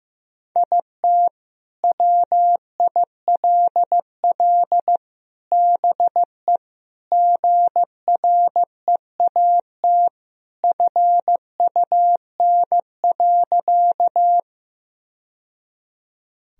Morse code, heard as a keyed tone, about 15 wpm, 700 Hz. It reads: IT WILL BE GREAT FUN.
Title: Sentences from Top 1000 Words - Review Part 4 - 15wpm